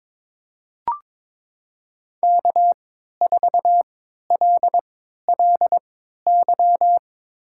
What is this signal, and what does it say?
Morse code, heard as a keyed tone, about 22 wpm, spaced as for 12 wpm, 700 Hz. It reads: K4LLY
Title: USA Call Signs - 5 Characters - 3x Character Spacing - 22wpm